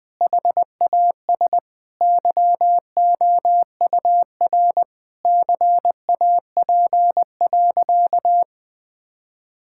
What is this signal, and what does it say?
Morse code, heard as a keyed tone, about 20 wpm, 700 Hz. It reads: HAS YOUR CAP.
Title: Sentences from Top 1600 Words 20wpm